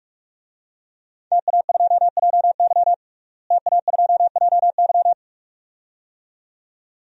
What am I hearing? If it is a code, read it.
Morse code, heard as a keyed tone, about 41 wpm, 700 Hz. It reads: TA2JY TA2JY
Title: DXCC-100 Call Signs - 5 Characters - Repeated 2x - 45wpm